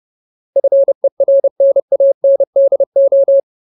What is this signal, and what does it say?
Morse code, heard as a keyed tone, about 30 wpm, 550 Hz. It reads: FERNANDO
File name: Ultimate Word List / 30WPM Ultimate